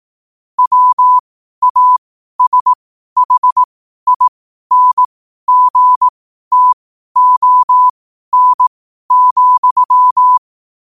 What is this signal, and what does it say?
Morse code, heard as a keyed tone, about 18 wpm, spaced as for 12 wpm, 1000 Hz. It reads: WASHINGTON,
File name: qotdfiles / 12wpm